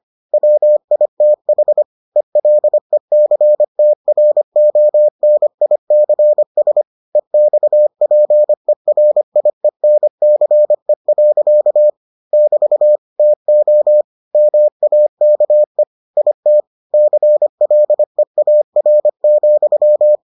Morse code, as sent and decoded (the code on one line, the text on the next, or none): .-- .. - .... / . .-.. -.-. - .-. --- -. .. -.-. ... / . -..- .--. .-. .. -. -.-. .-.-.- / -...- / - --- / -- .- -.- . / .. - / -.-. .-.. .- .-. --..--
WITH ELECTRONICS EXPERIENCE. <BT> TO MAKE IT CLEAR,